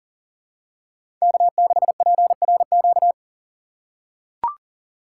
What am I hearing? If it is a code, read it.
Morse code, heard as a keyed tone, about 40 wpm, 700 Hz. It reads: K6PRQ
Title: USA Call Signs - 5 Characters - 2x Character Spacing - 40wpm